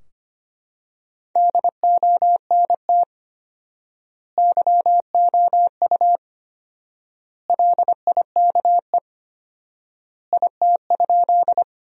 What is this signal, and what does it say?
Morse code, heard as a keyed tone, about 25 wpm, 700 Hz. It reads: DONT YOU LIKE IT?
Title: 4x Word-Spacing - Sentences from Top 300 Words - 25wpm